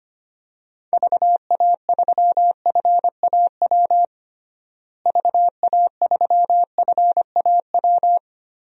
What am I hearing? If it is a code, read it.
Morse code, heard as a keyed tone, about 25 wpm, 700 Hz. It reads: VA3FAW VA3FAW
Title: DXCC-100 Call Signs - 6 Characters - Repeated 3x - 25wpm